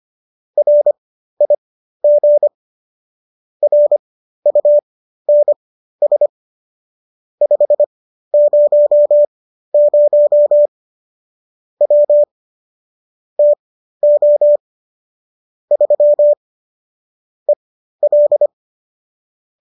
Morse code, heard as a keyed tone, about 25 wpm, spaced as for 13 wpm, 590 Hz. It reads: RIG RUNS 500 W TO 3 EL